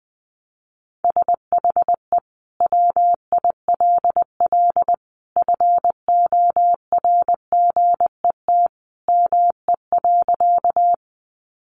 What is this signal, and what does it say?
Morse code, heard as a keyed tone, about 20 wpm, 700 Hz. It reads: SHE WILL FORGET ME.
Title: Sentences from Top 1200 Words 20wpm